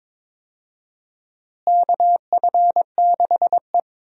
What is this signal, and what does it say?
Morse code, heard as a keyed tone, about 22 wpm, 700 Hz. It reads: KF6E
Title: USA Call Signs - 4 Characters - Repeated 2x - 22wpm